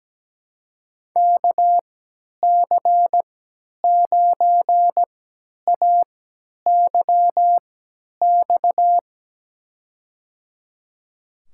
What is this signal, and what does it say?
Morse code, heard as a keyed tone, about 17 wpm, spaced as for 10 wpm, 700 Hz. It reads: KC9AYX
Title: USA Call Signs - 6 Characters - 3x Character Spacing - 17wpm